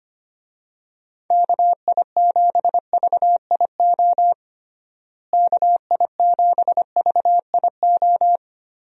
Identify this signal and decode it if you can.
Morse code, heard as a keyed tone, about 25 wpm, 700 Hz. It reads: KI7VIO KI7VIO
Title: USA Call Signs - 6 Characters - Repeated 3x - 25wpm